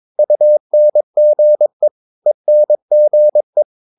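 Morse code, heard as a keyed tone, about 22 wpm, 600 Hz. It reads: UNGE ENGE
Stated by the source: Morse practice of short Danish words at 22 wpm, from CW med Gnister.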